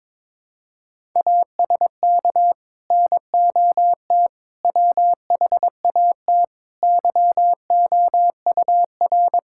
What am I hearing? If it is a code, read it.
Morse code, heard as a keyed tone, about 22 wpm, 700 Hz. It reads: ASK NOT WHAT YOUR
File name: practice / 22wpm / Sets of 8 Words - 22wpm